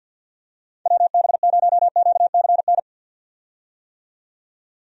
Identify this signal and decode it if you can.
Morse code, heard as a keyed tone, about 51 wpm, 700 Hz. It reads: WB0QCN
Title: USA Call Signs - 6 Characters - 50wpm